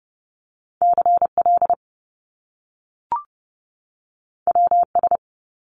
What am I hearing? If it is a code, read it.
Morse code, heard as a keyed tone, about 30 wpm, 700 Hz. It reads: CL WS
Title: Top Two Letter Combinations in Words 30wpm